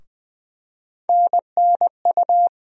Morse code, heard as a keyed tone, about 20 wpm, 700 Hz. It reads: NNU
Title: USA Call Sign Postfixes - 3 Letters 20wpm